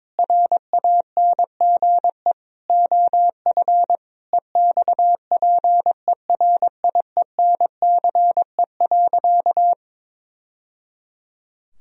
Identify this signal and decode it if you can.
Morse code, heard as a keyed tone, about 22 wpm, 700 Hz. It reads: RANGE OF EXPERIENCE.
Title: Sentences from Top 1000 Words - Review Part 4 - 22wpm